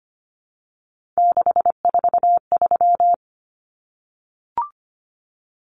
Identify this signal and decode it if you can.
Morse code, heard as a keyed tone, about 25 wpm, 700 Hz. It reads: 643
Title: Common Cut Numbers - 3-Digit 25wpm